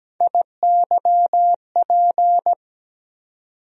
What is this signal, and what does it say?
Morse code, heard as a keyed tone, about 17 wpm, 700 Hz. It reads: IYP
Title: USA Call Sign Postfixes - 3 Letters 17wpm